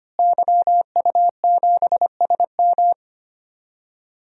Morse code, heard as a keyed tone, about 25 wpm, 700 Hz. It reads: YU7SM
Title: DXCC-100 Call Signs - 5 Characters - Repeated 3x - 25wpm